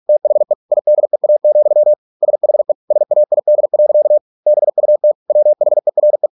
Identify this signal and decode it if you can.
Morse code, heard as a keyed tone, about 46 wpm, 600 Hz. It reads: THE IDEA, SHE SAID. BUT WHERE